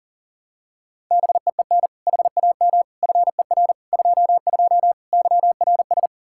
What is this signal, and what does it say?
Morse code, heard as a keyed tone, about 40 wpm, 700 Hz. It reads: BEEN HAM FER 22 YRS
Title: Practice QSO - Exchange 3 - More Info - 40wpm